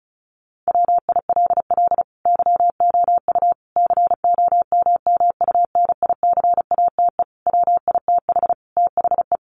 Morse code, heard as a keyed tone, about 34 wpm, 700 Hz. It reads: WILL YOU COMMUNICATE WITH THE
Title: Sentences from Top 1300 Words 35wpm